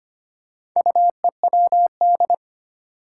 Morse code, heard as a keyed tone, about 25 wpm, 700 Hz. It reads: UEWD
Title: Four Challenging Letter-Number Combinations 25wpm